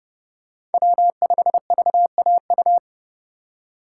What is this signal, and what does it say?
Morse code, heard as a keyed tone, about 31 wpm, 700 Hz. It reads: W5VAU
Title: USA Call Signs - 5 Characters - 3x Character Spacing - 30wpm